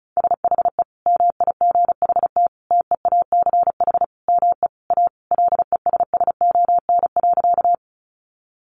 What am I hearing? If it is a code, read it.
Morse code, heard as a keyed tone, about 34 wpm, 700 Hz. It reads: SHE MIGHT TEACH ME A LESSON.
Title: Sentences from Top 1200 Words 35wpm